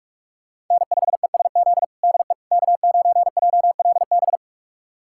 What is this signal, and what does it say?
Morse code, heard as a keyed tone, about 42 wpm, 700 Hz. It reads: N5ESZ DE K9JLB
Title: USA Callsign DE Callsign - 5 Characters - 45wpm